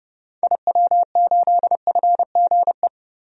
Morse code, heard as a keyed tone, about 30 wpm, 700 Hz. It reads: IW8FGE
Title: DXCC-100 Call Signs - 6 Characters - Repeated 2x - 30wpm